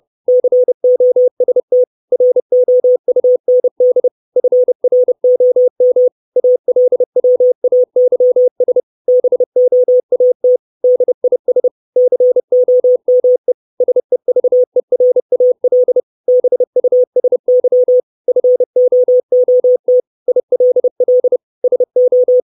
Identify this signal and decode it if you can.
Morse code, heard as a keyed tone, about 30 wpm, 500 Hz. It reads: COST ROUND FROM ALWAYS BOAT DIS COME SEVERAL BUSY FOOT ILL SO
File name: 632 Words 30WPM